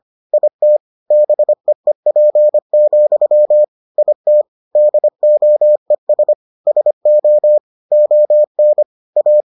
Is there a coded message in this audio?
Yes: Morse code from 0.3 to 9.4 s, heard as a keyed tone, about 25 wpm, 600 Hz. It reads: IT BEEP, IT DOES SO ON A